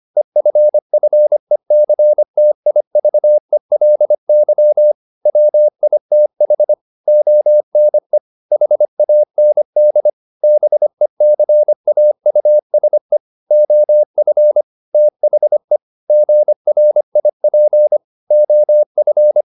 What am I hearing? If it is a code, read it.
Morse code, heard as a keyed tone, about 25 wpm, 600 Hz. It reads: EFFECTIVELY WITH ONE HAND BECAUSE OF THE GRIP OF